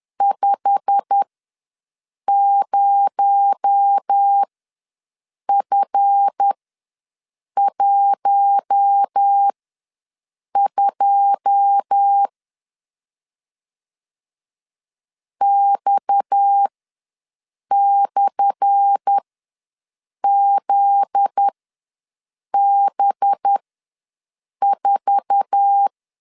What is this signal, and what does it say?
Morse code, heard as a keyed tone, about 10 wpm, spaced as for 6 wpm, 800 Hz. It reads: 50F12 X/ZB4
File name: LEZIONI / 10 SIGNS / 10A SLASH ws 10wpm 4121